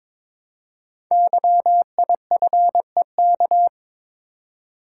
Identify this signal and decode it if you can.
Morse code, heard as a keyed tone, about 22 wpm, 700 Hz. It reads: YIFEK
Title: Code Groups - 5 Letters - 22wpm